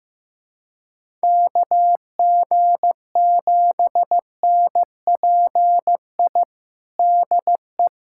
Morse code, heard as a keyed tone, about 15 wpm, 700 Hz. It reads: KG7NPI DE
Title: USA Callsign DE Callsign - 6 Characters - 15wpm